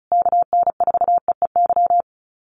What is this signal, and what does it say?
Morse code, heard as a keyed tone, about 34 wpm, 700 Hz. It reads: KN4EEY
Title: USA Call Signs Repeated 3x 35wpm